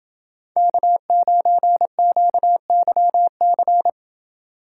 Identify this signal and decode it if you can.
Morse code, heard as a keyed tone, about 29 wpm, 700 Hz. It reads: K9QYC